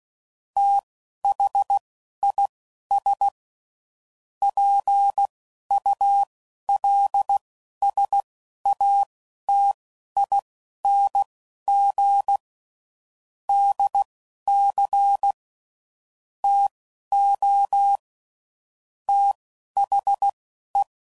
Morse code, heard as a keyed tone, about 16 wpm, spaced as for 11 wpm, 800 Hz. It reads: THIS PULSATING DC TO THE
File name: CW Code Groups @10WPM / Technical Subjects / SwPwrSup